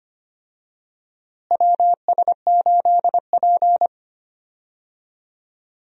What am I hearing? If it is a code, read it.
Morse code, heard as a keyed tone, about 25 wpm, 700 Hz. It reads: WS8P
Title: USA Call Signs - 4 Characters - Repeated 2x - 25wpm